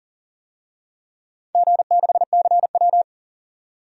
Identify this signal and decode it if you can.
Morse code, heard as a keyed tone, about 41 wpm, 700 Hz. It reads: G6CW
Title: DXCC-100 Call Signs - 4 Characters - 40wpm